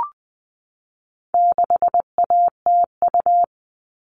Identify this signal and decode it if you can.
Morse code, heard as a keyed tone, about 20 wpm, 700 Hz. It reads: 6ATU
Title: All Cut Numbers - 4-Digit 20wpm